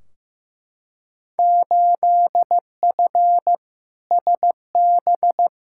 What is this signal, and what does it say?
Morse code, heard as a keyed tone, about 15 wpm, 700 Hz. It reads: 8F SB
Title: Field Day Exchanges Encore - 15wpm